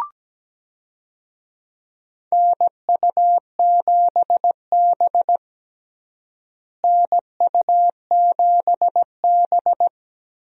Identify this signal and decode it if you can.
Morse code, heard as a keyed tone, about 17 wpm, 700 Hz. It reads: NU7B NU7B
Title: USA Call Signs - 4 Characters - Repeated 3x - 17wpm